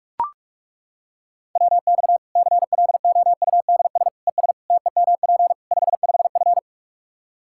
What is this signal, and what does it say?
Morse code, heard as a keyed tone, about 42 wpm, 700 Hz. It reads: WX CLOUDS ES TEMP 55F
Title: K1USN SST QSO Practice 45wpm